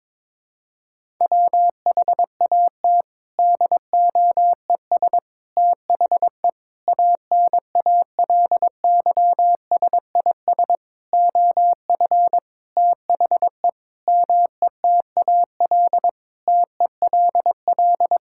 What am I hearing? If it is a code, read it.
Morse code, heard as a keyed tone, about 22 wpm, 700 Hz. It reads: WHAT DOES THE ANALYSIS OF THE METAL TELL
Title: Sentences from Top 1000 Words - Review Part 6 - 22wpm